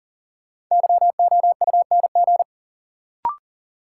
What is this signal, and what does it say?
Morse code, heard as a keyed tone, about 40 wpm, 700 Hz. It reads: YOUNG
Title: Sentences from Top 400 Words - New Word Drill - 40wpm